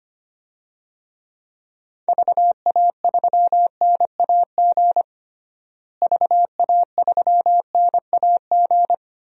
Morse code, heard as a keyed tone, about 25 wpm, 700 Hz. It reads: VA3NAG VA3NAG
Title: DXCC-100 Call Signs - 6 Characters - Repeated 3x - 25wpm